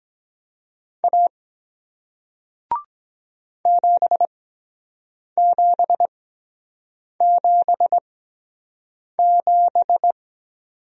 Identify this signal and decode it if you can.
Morse code, heard as a keyed tone, about 23 wpm, spaced as for 7 wpm, 700 Hz. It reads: A 7777